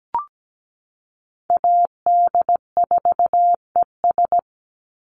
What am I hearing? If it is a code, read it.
Morse code, heard as a keyed tone, about 17 wpm, 700 Hz. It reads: AD4ES
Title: Call Signs Repeated 3x - 17wpm